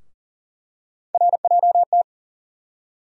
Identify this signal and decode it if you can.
Morse code, heard as a keyed tone, about 40 wpm, 700 Hz. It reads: RJT
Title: USA Call Sign Postfixes - 3 Letters 40wpm